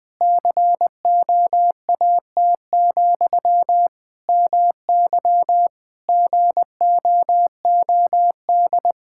Morse code, heard as a keyed tone, about 20 wpm, 700 Hz. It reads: COAT, MY GOOD
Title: Sentences from Top 1600 Words 20wpm